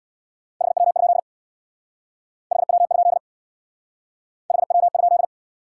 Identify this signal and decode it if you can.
Morse code, heard as a keyed tone, about 67 wpm, 700 Hz. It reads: HW? HW? HW?